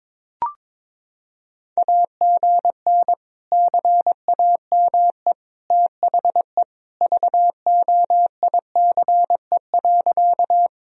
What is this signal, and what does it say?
Morse code, heard as a keyed tone, about 22 wpm, 700 Hz. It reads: AGN CAME THE VOICE.
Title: Sentences from Top 500 Words and Abbreviations 22wpm